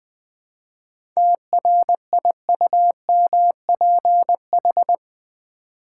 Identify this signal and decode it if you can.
Morse code, heard as a keyed tone, about 20 wpm, 700 Hz. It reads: TRIUMPH
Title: 7 Letter Words - First 3 Letters Given - 20wpm